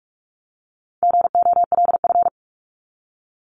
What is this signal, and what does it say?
Morse code, heard as a keyed tone, about 44 wpm, 700 Hz. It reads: GOLF
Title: Top 1751-2000 Words 45wpm